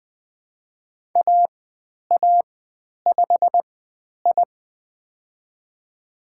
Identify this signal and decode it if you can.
Morse code, heard as a keyed tone, about 20 wpm, spaced as for 10 wpm, 700 Hz. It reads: AA5I
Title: 095 Lesson 16g - Letter c - Call Signs 20wpm